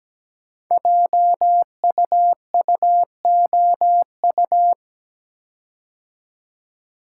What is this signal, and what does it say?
Morse code, heard as a keyed tone, about 17 wpm, 700 Hz. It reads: JUUOU
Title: Code Groups - 5 Letters - 17wpm